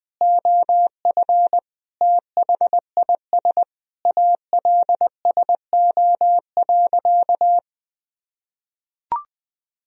Morse code, heard as a keyed tone, about 19 wpm, 700 Hz. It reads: OF THIS ALSO.
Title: Sentences from Top 1000 Words - Review Part 1 - 20wpm